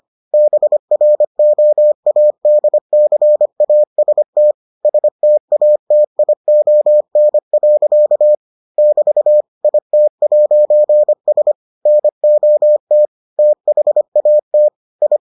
Morse code, heard as a keyed tone, about 25 wpm, 600 Hz. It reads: BROADCAST STATION. <BT> IT'S NOT THAT I